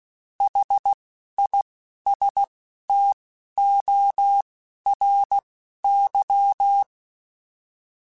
Morse code, heard as a keyed tone, about 16 wpm, spaced as for 12 wpm, 800 Hz. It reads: HISTORY
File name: T then H Runs / T then H Runs.mp3s / T then H Runs @10